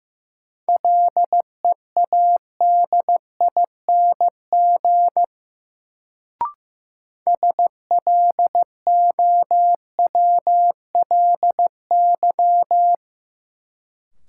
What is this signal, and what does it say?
Morse code, heard as a keyed tone, about 15 wpm, 700 Hz. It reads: LEADING SLOWLY
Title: Sentences from Top 1100 Words - New Word Drill Alt - 15wpm